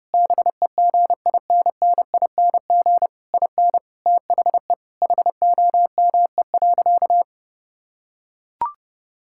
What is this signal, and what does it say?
Morse code, heard as a keyed tone, about 30 wpm, 700 Hz. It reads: BEGINNING IN THE HOME.
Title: Sentences from Top 1000 Words - Review Part 6 - 30wpm